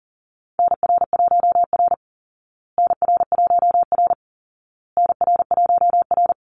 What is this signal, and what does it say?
Morse code, heard as a keyed tone, about 40 wpm, 700 Hz. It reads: NR1R NR1R NR1R
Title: Call Signs Repeated 2x - 40wpm